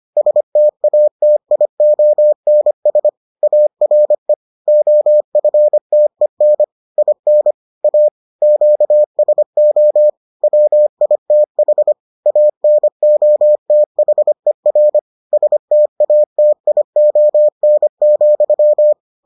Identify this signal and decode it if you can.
Morse code, heard as a keyed tone, about 25 wpm, 600 Hz. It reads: STATIONS ARE OFTEN IN A QSO WITH ANOTHER STATION,